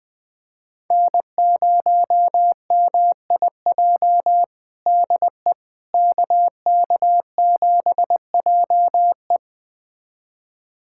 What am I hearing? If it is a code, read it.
Morse code, heard as a keyed tone, about 21 wpm, 700 Hz. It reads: N0MIJ DE KK7JE
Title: USA Callsign DE Callsign - 5 Characters - 20wpm